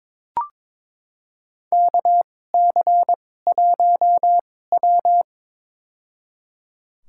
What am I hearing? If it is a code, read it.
Morse code, heard as a keyed tone, about 22 wpm, spaced as for 16 wpm, 700 Hz. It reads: KC1W
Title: USA Call Signs - 2x Character Spacing - 22wpm